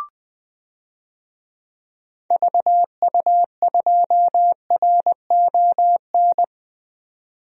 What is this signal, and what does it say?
Morse code, heard as a keyed tone, about 20 wpm, 700 Hz. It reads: VU2RON
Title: DXCC-100 Call Signs - 6 Characters - 20wpm